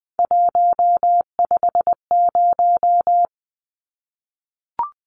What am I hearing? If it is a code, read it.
Morse code, heard as a keyed tone, about 19 wpm, 700 Hz. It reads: 150
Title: Numbers - 3-Digit 20wpm